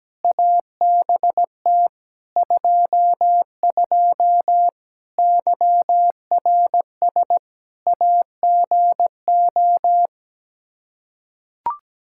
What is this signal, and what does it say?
Morse code, heard as a keyed tone, about 17 wpm, 700 Hz. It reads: ABT 22 YRS AGO
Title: Practice QSO - Exchange 3 - More Info - 17wpm